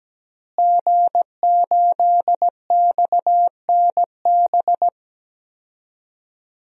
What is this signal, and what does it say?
Morse code, heard as a keyed tone, about 17 wpm, 700 Hz. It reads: G8XNB
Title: DXCC-100 Call Signs - 5 Characters - Repeated 2x - 17wpm